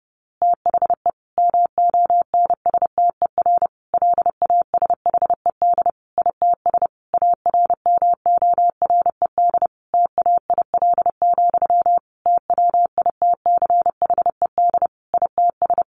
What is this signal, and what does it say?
Morse code, heard as a keyed tone, about 30 wpm, 700 Hz. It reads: THE MONSTER LASHED ITS ARMORED TAIL, TWITCHED ITS
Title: A Sound Of Thunder 30wpm